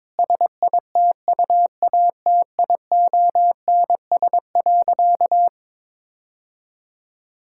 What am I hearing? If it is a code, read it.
Morse code, heard as a keyed tone, about 22 wpm, 700 Hz. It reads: SITUATIONS.
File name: Difficult Sentences with Hints - 22wpm